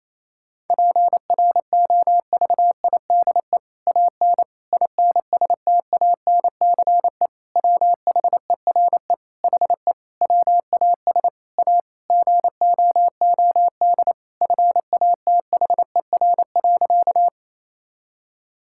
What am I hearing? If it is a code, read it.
Morse code, heard as a keyed tone, about 28 wpm, 700 Hz. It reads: PROVIDE AN INSTANCE WHERE HE WAS A GOOD FATHER.
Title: Sentences from Top 1000 Words - Review Part 5 - 28wpm